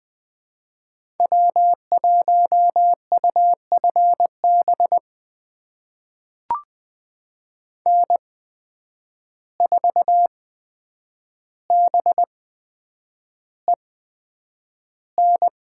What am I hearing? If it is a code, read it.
Morse code, heard as a keyed tone, about 20 wpm, spaced as for 9 wpm, 700 Hz. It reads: W1UFB N4BEN
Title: USA Call Signs - 5 Characters - 8x Character Spacing - 20wpm